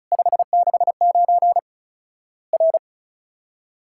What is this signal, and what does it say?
Morse code, heard as a keyed tone, about 36 wpm, 700 Hz. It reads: 569 R
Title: Worked All Europe DX Contest - QTC Exchange - 35wpm